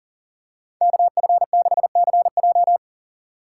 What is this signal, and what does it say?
Morse code, heard as a keyed tone, about 40 wpm, 700 Hz. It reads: KF6CJ
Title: USA Call Signs - 5 Characters - 4x Character Spacing - 40wpm